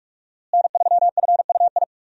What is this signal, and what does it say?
Morse code, heard as a keyed tone, about 42 wpm, 700 Hz. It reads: N3FUI
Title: USA Call Signs - 5 Characters - Repeated 2x - 45wpm